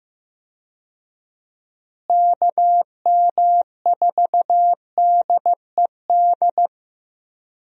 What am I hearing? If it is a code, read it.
Morse code, heard as a keyed tone, about 15 wpm, 700 Hz. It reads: KM4DED
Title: USA Call Signs - 6 Characters - Repeated 3x - 15wpm